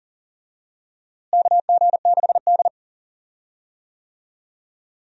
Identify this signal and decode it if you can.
Morse code, heard as a keyed tone, about 40 wpm, 700 Hz. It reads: KG6D
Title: USA Call Signs - 4 Characters - Repeated 2x - 40wpm